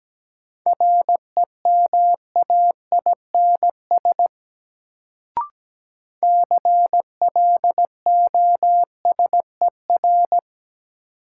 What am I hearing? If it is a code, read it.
Morse code, heard as a keyed tone, about 17 wpm, 700 Hz. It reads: REMAINS CLOSER
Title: Sets of 5 Words Encore - New Word Drill - 17wpm